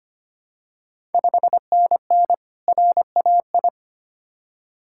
Morse code, heard as a keyed tone, about 25 wpm, 700 Hz. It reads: 5NN RAI
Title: Ontario QSO Party Exchange 25wpm